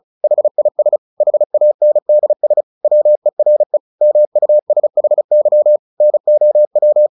Morse code, heard as a keyed tone, about 34 wpm, 600 Hz. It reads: HIS HANDS WERE MUSHY NOW
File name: the old man and the sea 2 35wpm